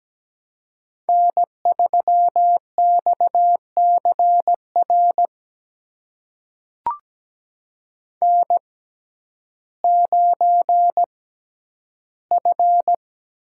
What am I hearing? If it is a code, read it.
Morse code, heard as a keyed tone, about 17 wpm, spaced as for 10 wpm, 700 Hz. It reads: N3XCR N9F
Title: USA Call Signs - 5 Characters - 6x Character Spacing - 17wpm